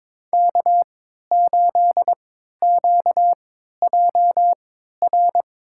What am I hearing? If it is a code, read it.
Morse code, heard as a keyed tone, about 21 wpm, spaced as for 12 wpm, 700 Hz. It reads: K8QJR
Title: USA Call Signs - 5 Characters - 3x Character Spacing - 22wpm